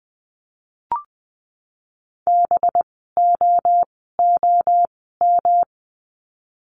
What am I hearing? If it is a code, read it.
Morse code, heard as a keyed tone, about 20 wpm, spaced as for 15 wpm, 700 Hz. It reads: BOOM
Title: 185 Lesson 30f - Letter b - Words 20wpm